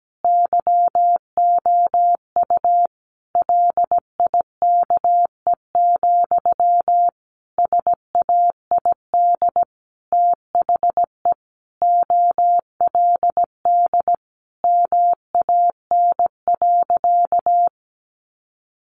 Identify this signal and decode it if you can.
Morse code, heard as a keyed tone, about 17 wpm, 700 Hz. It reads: YOU LIKE, SAID THE OLD MAN.
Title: Sentences from Top 1100 Words 17wpm